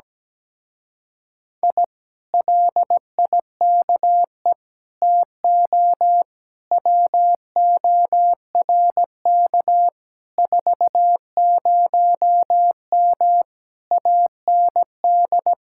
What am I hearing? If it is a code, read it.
Morse code, heard as a keyed tone, about 17 wpm, 700 Hz. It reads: I LIKE TO WORK 40M AND
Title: Speed-Racing Common QSO Phrases Encore - 17wpm